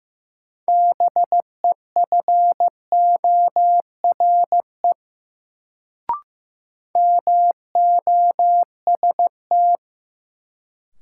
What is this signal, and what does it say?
Morse code, heard as a keyed tone, about 15 wpm, 700 Hz. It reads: BEFORE MOST
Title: Sentences from Top 200 Words - New Word Drill - 15wpm